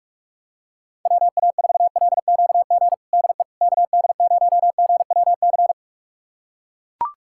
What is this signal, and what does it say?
Morse code, heard as a keyed tone, about 42 wpm, 700 Hz. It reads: WA4LQG DE KD0GWC
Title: USA Callsign DE Callsign - 6 Characters - 45wpm